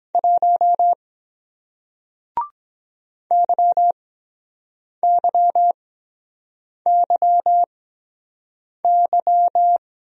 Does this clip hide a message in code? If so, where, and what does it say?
Yes: Morse code from 0.1 to 9.8 s, heard as a keyed tone, about 23 wpm, spaced as for 7 wpm, 700 Hz. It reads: 1 YYYY